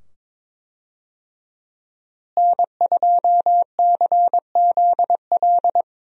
Morse code, heard as a keyed tone, about 22 wpm, 700 Hz. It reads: N2CZL
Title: USA Call Signs - 5 Characters - Repeated 2x - 22wpm